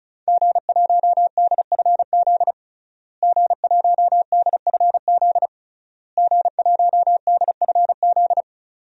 Morse code, heard as a keyed tone, about 36 wpm, 700 Hz. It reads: G1DFZ G1DFZ G1DFZ
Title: DXCC-100 Call Signs - 5 Characters - Repeated 3x - 35wpm